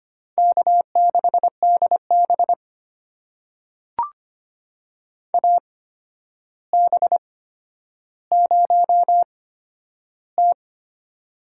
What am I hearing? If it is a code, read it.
Morse code, heard as a keyed tone, about 25 wpm, spaced as for 11 wpm, 700 Hz. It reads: K6DB AB0T
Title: USA Call Signs - 4 Characters - 8x Character Spacing - 25wpm